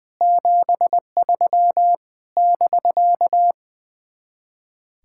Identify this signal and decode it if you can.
Morse code, heard as a keyed tone, about 20 wpm, 700 Hz. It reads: 73 <BK>
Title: Speed-Racing Common QSO Phrases - 20wpm